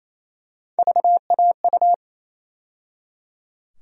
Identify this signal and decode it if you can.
Morse code, heard as a keyed tone, about 28 wpm, 700 Hz. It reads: VAU
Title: USA Call Sign Postfixes - 3 Letters 28wpm